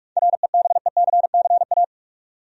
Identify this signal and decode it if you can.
Morse code, heard as a keyed tone, about 43 wpm, 700 Hz. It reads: REBECCA
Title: USA Common Names - 45wpm